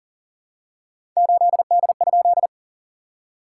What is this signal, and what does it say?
Morse code, heard as a keyed tone, about 38 wpm, 700 Hz. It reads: 8D?
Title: SOTA QSO Practice 40wpm